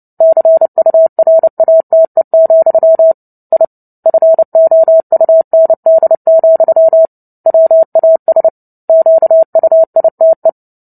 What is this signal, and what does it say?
Morse code, heard as a keyed tone, about 30 wpm, 650 Hz. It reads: CURATE, I FOUND, WAS QUITE
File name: Book Two / 03-Chapter 3 - 29 WPM